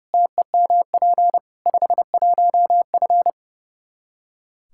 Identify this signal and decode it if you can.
Morse code, heard as a keyed tone, about 30 wpm, 700 Hz. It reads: TEMP 51F
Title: Practice QSO - Exchange 2 - Info - 30wpm